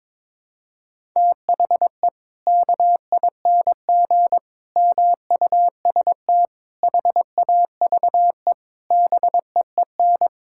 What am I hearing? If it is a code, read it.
Morse code, heard as a keyed tone, about 22 wpm, 700 Hz. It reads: THE KING MUST HAVE BEEN
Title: Sentences from Top 1000 Words - Review Part 5 - 22wpm